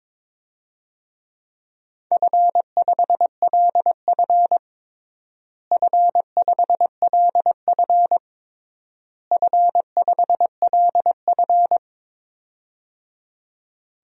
Morse code, heard as a keyed tone, about 22 wpm, 700 Hz. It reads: F5LF F5LF F5LF
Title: DXCC-100 Call Signs - 4 Characters - Repeated 3x - 22wpm